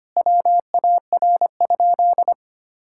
Morse code, heard as a keyed tone, about 25 wpm, 700 Hz. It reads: WAR?
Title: Sentences from Top 1000 Words - Review Part 5 - 25wpm